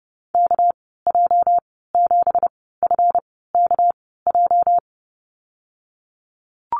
Morse code, heard as a keyed tone, about 30 wpm, spaced as for 17 wpm, 700 Hz. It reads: KJ7FKJ
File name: Lesson 37g - Letter x - Call Signs 30wpm